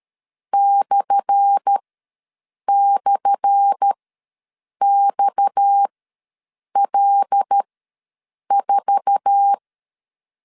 Morse code, heard as a keyed tone, about 13 wpm, spaced as for 7 wpm, 800 Hz. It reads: //XL4
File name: LEZIONI / 10 SIGNS / 10A SLASH ws 12wpm 3443